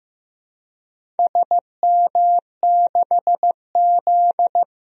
Morse code, heard as a keyed tone, about 15 wpm, 700 Hz. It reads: SM6Z
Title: DXCC-100 Call Signs - 4 Characters - Repeated 2x - 15wpm